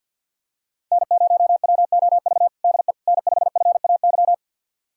Morse code, heard as a keyed tone, about 50 wpm, 700 Hz. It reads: N0WOV DE N5FAY
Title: USA Callsign DE Callsign - 5 Characters - 50wpm